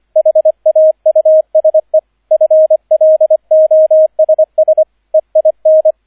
Morse code, heard as a keyed tone, about 25 wpm, 630 Hz, with radio noise. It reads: HAUSE FLOSS EIN
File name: 003 Dritter Streich 25wpm